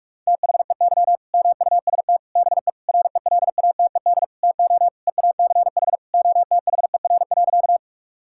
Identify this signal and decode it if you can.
Morse code, heard as a keyed tone, about 41 wpm, 700 Hz. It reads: THEY MUST BE RELATED TO EACH OTHER.
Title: Sentences from Top 1000 Words - Review Part 5 - 45wpm